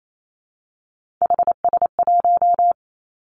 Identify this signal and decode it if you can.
Morse code, heard as a keyed tone, about 28 wpm, 700 Hz. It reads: HS1
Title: Three Challenging Letter-Number Combinations 28wpm